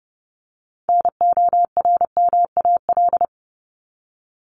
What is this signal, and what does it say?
Morse code, heard as a keyed tone, about 30 wpm, 700 Hz. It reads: NORMAL
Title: Top 1001-1250 Words 30wpm